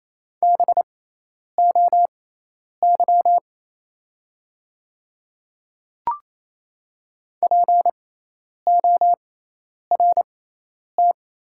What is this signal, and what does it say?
Morse code, heard as a keyed tone, about 28 wpm, spaced as for 10 wpm, 700 Hz. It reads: BOY PORT
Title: 6x Character Spacing - Top 300 Words - 28wpm